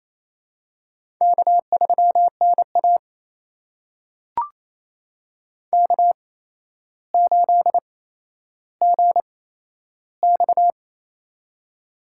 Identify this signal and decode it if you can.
Morse code, heard as a keyed tone, about 27 wpm, spaced as for 12 wpm, 700 Hz. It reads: K3NA K8GX
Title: USA Call Signs - 8x Character Spacing - 28wpm